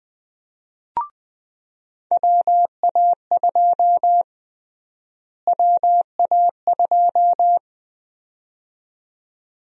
Morse code, heard as a keyed tone, about 20 wpm, 700 Hz. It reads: WA2 WA2
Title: USA Call Sign Prefixes - Repeated 2x - 20wpm